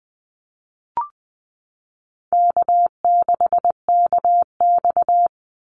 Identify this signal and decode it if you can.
Morse code, heard as a keyed tone, about 20 wpm, 700 Hz. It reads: K6KX
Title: Easy Call Signs Repeated 2x - 20wpm